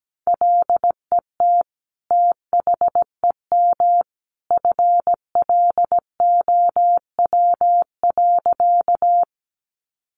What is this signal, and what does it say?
Morse code, heard as a keyed tone, about 17 wpm, 700 Hz. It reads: LET THEM FLOW.
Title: Sentences from Top 1200 Words 17wpm